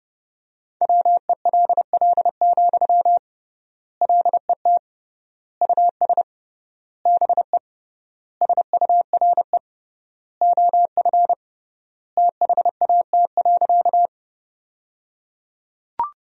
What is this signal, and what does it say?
Morse code, heard as a keyed tone, about 30 wpm, 700 Hz. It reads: WELL, LET US BE SURE OF THAT.
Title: 3x Word-Spacing - Sentences from Top 300 Words - 30wpm